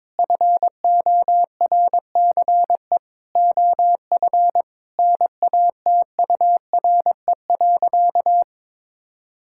Morse code, heard as a keyed tone, about 22 wpm, 700 Hz. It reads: FORCE OF NATURE.
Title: Sentences from Top 1000 Words - Review Part 5 - 22wpm